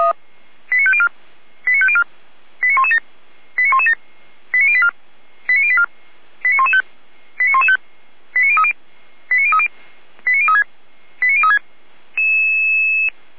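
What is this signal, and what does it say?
Five-tone selective call (ZVEI), heard as quick runs of five tones, tones of about 70 ms. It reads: 89504 89504 89117 89117 89084 89084 89115 89115 89030 89030 89036 89036
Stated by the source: ZVEI five-tone selective call received by SDR.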